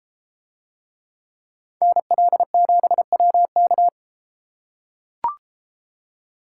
Morse code, heard as a keyed tone, about 32 wpm, 700 Hz. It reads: NL7WK